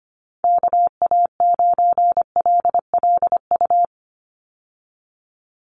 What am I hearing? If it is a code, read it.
Morse code, heard as a keyed tone, about 25 wpm, 700 Hz. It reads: KA9LLU
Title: USA Call Signs 25wpm